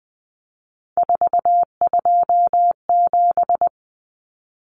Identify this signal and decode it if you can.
Morse code, heard as a keyed tone, about 20 wpm, 700 Hz. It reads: 427
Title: Numbers - 3-Digit 20wpm